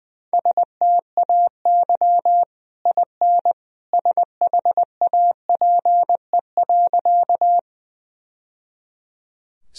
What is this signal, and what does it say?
Morse code, heard as a keyed tone, about 20 wpm, 700 Hz. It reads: STAY IN SHAPE.
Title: Sentences from Top 1000 Words - Review Part 3 - 20wpm